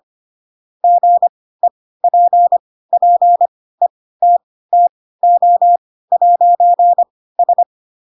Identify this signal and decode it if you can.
Morse code, heard as a keyed tone, about 25 wpm, spaced as for 16 wpm, 700 Hz. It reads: GEPPETTO'S
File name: Pinocchio / Pinocchio ch2 10 25 ik1hge